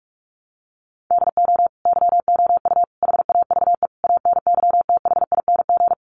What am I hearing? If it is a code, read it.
Morse code, heard as a keyed tone, about 44 wpm, 700 Hz. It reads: DO YOU HAVE ANYTHING
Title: Sentences from Top 1000 Words 45wpm